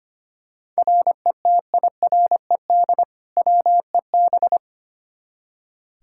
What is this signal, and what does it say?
Morse code, heard as a keyed tone, about 25 wpm, 700 Hz. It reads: RETIRED WEB
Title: K1USN SST QSO Practice 25wpm